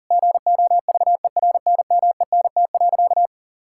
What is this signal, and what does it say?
Morse code, heard as a keyed tone, about 40 wpm, 700 Hz. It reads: GOVERNMENT.
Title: Sentences from Top 1000 Words - Review Part 5 - 40wpm